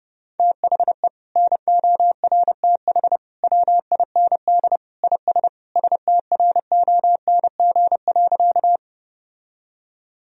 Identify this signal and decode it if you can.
Morse code, heard as a keyed tone, about 30 wpm, 700 Hz. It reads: THE NORTH WIND IS STRONG.
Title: Sentences from Top 1000 Words - Review Part 2 - 30wpm